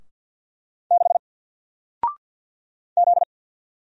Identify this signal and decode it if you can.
Morse code, heard as a keyed tone, about 48 wpm, 700 Hz. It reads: 6 Z
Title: Single Letter-Number Alt 50wpm